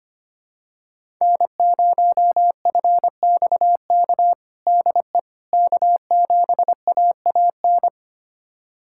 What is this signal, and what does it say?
Morse code, heard as a keyed tone, about 25 wpm, 700 Hz. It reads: N0FXK DE K7AAN
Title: USA Callsign DE Callsign - 5 Characters - 25wpm